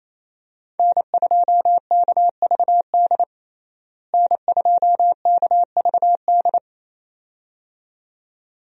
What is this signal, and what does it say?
Morse code, heard as a keyed tone, about 28 wpm, 700 Hz. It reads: N2KVD N2KVD
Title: USA Call Signs - 5 Characters - Repeated 2x - 28wpm